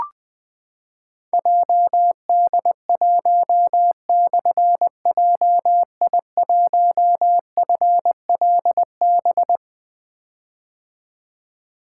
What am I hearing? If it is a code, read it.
Morse code, heard as a keyed tone, about 20 wpm, 700 Hz. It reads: JD1/JI1FLB
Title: Difficult Call Signs 20wpm